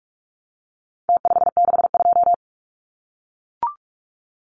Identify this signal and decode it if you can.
Morse code, heard as a keyed tone, about 44 wpm, 700 Hz. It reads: T562
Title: Numbers - Contest Style - 4-Digit 45wpm